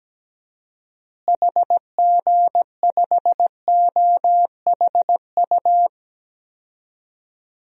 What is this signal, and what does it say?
Morse code, heard as a keyed tone, about 17 wpm, 700 Hz. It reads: HG5OHU
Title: DXCC-100 Call Signs - 6 Characters - Repeated 3x - 17wpm